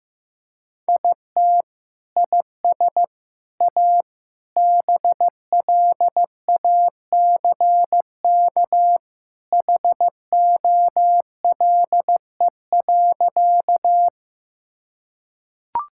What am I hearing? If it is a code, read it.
Morse code, heard as a keyed tone, about 15 wpm, 700 Hz. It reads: IT IS A BLACK HOLE.
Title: Sentences from Top 1600 Words 15wpm